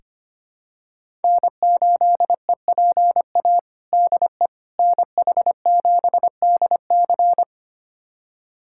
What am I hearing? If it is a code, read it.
Morse code, heard as a keyed tone, about 25 wpm, 700 Hz. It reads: N8EPA DE NH7DC
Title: USA Callsign DE Callsign - 5 Characters - 25wpm